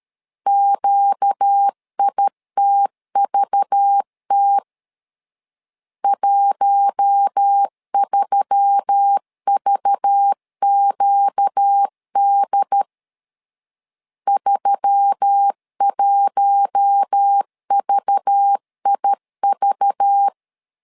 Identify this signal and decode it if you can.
Morse code, heard as a keyed tone, about 13 wpm, 800 Hz. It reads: QITVT 13VQD 31VIV
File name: LEZIONI / 5 LBVC / 5 LBV 12wpm 1855